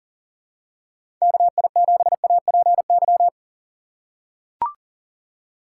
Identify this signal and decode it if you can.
Morse code, heard as a keyed tone, about 39 wpm, 700 Hz. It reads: KI7APY
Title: USA Call Signs - 6 Characters - 6x Character Spacing - 40wpm